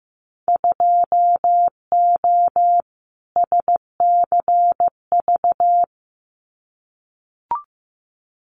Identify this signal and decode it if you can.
Morse code, heard as a keyed tone, about 15 wpm, 700 Hz. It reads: 2O SCV
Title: Winter Field Day Exchanges Encore 15wpm